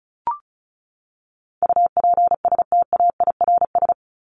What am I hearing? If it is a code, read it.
Morse code, heard as a keyed tone, about 34 wpm, 700 Hz. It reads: UPSTAIRS
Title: Top 1751-2000 Words 35wpm